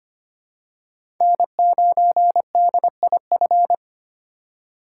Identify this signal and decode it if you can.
Morse code, heard as a keyed tone, about 25 wpm, 700 Hz. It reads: N9DIF
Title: USA Call Signs - 5 Characters - 4x Character Spacing - 25wpm